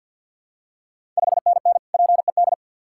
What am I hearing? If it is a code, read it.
Morse code, heard as a keyed tone, about 50 wpm, 700 Hz. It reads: 5NN PED
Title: Ontario QSO Party Exchange 50wpm